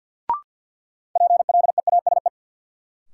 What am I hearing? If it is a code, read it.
Morse code, heard as a keyed tone, about 51 wpm, 700 Hz. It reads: PLEASE
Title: Sentences from Top 1500 Words - New Word Drill Alt - 50wpm